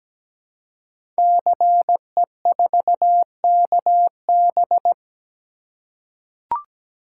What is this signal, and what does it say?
Morse code, heard as a keyed tone, about 17 wpm, 700 Hz. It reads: CE4KB
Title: Code Groups - 5 Letter:Numbers - 17wpm